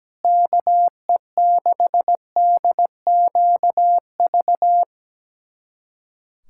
Morse code, heard as a keyed tone, about 17 wpm, 700 Hz. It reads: KE6DQV
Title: USA Callsign DE Callsign - 6 Characters - 17wpm